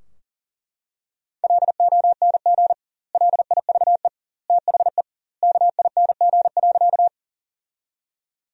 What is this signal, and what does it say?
Morse code, heard as a keyed tone, about 40 wpm, 700 Hz. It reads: LONG LIVE THE KING.
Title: 2x Word-Spacing - Sentences from Top 400 Words - 40wpm